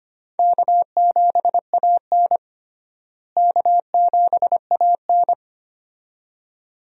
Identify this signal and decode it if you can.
Morse code, heard as a keyed tone, about 25 wpm, 700 Hz. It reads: K7AN K7AN
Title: USA Call Signs - 4 Characters - Repeated 2x - 25wpm